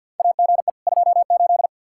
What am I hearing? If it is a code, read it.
Morse code, heard as a keyed tone, about 49 wpm, 700 Hz. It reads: AGE 28
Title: K1USN SST QSO Practice 50wpm